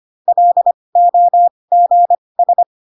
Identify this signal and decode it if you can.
Morse code, heard as a keyed tone, about 25 wpm, spaced as for 20 wpm, 700 Hz. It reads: LOGS
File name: Pinocchio / Pinocchio ch1 15 25 ik1hge